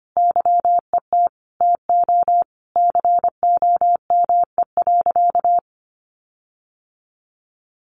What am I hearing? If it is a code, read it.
Morse code, heard as a keyed tone, about 25 wpm, 700 Hz. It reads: YET TO COME.
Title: Sentences from Top 1500 Words 25wpm